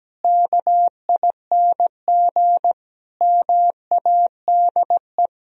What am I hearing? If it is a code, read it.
Morse code, heard as a keyed tone, about 17 wpm, 700 Hz. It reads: KING MADE
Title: Sentences from Top 1000 Words - Review Part 4 - 17wpm